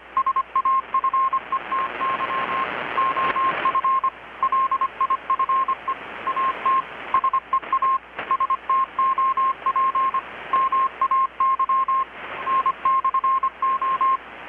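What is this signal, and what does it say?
Morse code, heard as a keyed tone, about 25 wpm, 1060 Hz, with radio noise. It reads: SAFETY OF LIFE AT SEA STOP MAY N/O
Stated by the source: Morse from KPH's Night of Nights XX maritime radiotelegraphy event, received in Chiapas, Mexico about 3,800 km away on a Kenwood R-600 with a grounded 10 m long-wire antenna.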